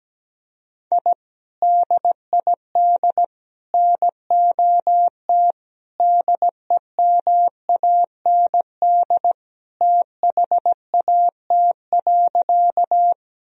Morse code, heard as a keyed tone, about 17 wpm, 700 Hz. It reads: I DID NOT DEMAND THAT.
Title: Sentences from Top 1000 Words - Review Part 4 - 17wpm